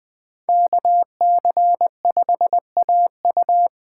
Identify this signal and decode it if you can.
Morse code, heard as a keyed tone, about 20 wpm, 700 Hz. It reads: KC5AU
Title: USA Call Signs - 5 Characters - 4x Character Spacing - 20wpm